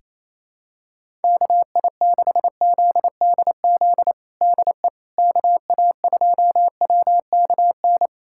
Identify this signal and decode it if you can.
Morse code, heard as a keyed tone, about 27 wpm, 700 Hz. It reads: KI6ZDZ DE KA2WKN
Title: USA Callsign DE Callsign 28wpm